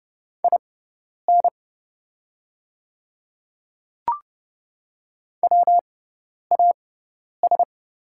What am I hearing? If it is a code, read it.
Morse code, heard as a keyed tone, about 30 wpm, spaced as for 10 wpm, 700 Hz. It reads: IN WAS
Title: 6x Character Spacing - Top 100 Words - 30wpm